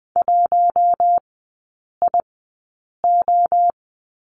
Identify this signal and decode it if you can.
Morse code, heard as a keyed tone, about 20 wpm, spaced as for 8 wpm, 700 Hz. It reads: 1IO
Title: 041 Lesson 8e - Number 1 - Three Characters 20wpm